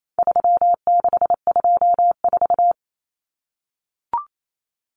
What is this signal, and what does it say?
Morse code, heard as a keyed tone, about 28 wpm, 700 Hz. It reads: 3624
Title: Common Cut Numbers - 4-Digit 28wpm